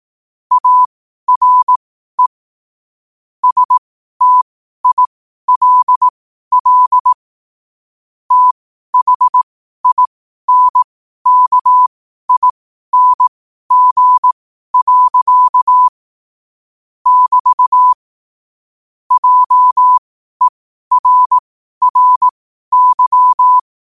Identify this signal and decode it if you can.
Morse code, heard as a keyed tone, about 18 wpm, spaced as for 12 wpm, 1000 Hz. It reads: ARE STILL THINKING. <BT> JERRY
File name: qotdfiles / 12wpm